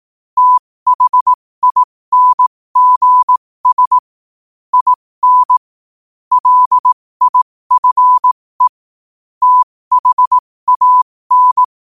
Morse code, heard as a keyed tone, about 18 wpm, spaced as for 15 wpm, 1000 Hz. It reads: THINGS IN LIFE THAN
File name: qotdfiles / 15wpm